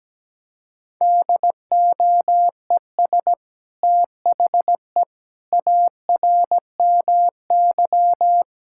Morse code, heard as a keyed tone, about 17 wpm, 700 Hz. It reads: DOES THE ARMY
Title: Sentences from Top 1000 Words - Review Part 5 - 17wpm